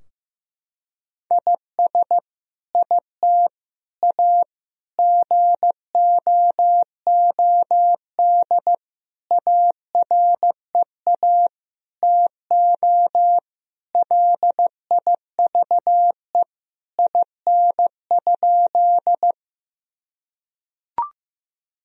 Morse code, heard as a keyed tone, about 15 wpm, 700 Hz. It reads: IS IT A GOOD AREA TO LIVE IN?
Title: Sentences from Top 1000 Words - Review Part 2 - 15wpm